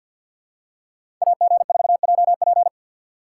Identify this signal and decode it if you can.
Morse code, heard as a keyed tone, about 47 wpm, 700 Hz. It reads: AG4JP
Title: USA Call Signs - 5 Characters - 3x Character Spacing - 50wpm